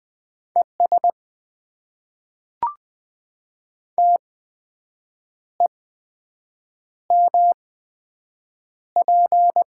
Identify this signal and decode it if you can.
Morse code, heard as a keyed tone, about 20 wpm, spaced as for 7 wpm, 700 Hz. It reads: ES TEMP
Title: Top 90 QSO Elements - 8x Character Spacing - 20wpm